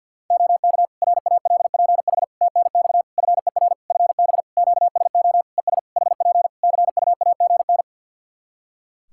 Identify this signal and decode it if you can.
Morse code, heard as a keyed tone, about 49 wpm, 700 Hz. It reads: OK RALPH TNX FER FB QSO ES HP CUAGN
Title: Practice QSO - Exchange 4 - Ending - 50wpm